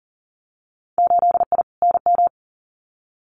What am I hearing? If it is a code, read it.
Morse code, heard as a keyed tone, about 39 wpm, 700 Hz. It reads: 8I NM
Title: Winter Field Day Exchanges Encore 40wpm